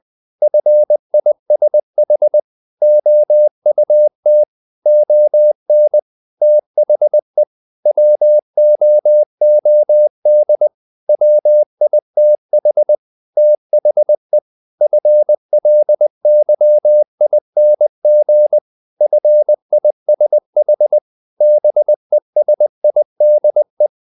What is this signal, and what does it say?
Morse code, heard as a keyed tone, about 20 wpm, 600 Hz. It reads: FISH OUT ON THE WOOD WITH THE FLYING FISH BESIDE
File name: the old man and the sea part 2 20wpm